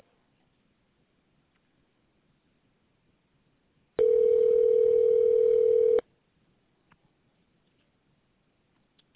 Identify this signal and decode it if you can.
Ringback tone (North America), 440 + 480 Hz.